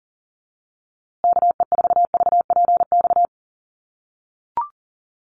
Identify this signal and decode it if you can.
Morse code, heard as a keyed tone, about 40 wpm, 700 Hz. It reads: KE4VPX
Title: USA Call Signs Repeated 2x 40wpm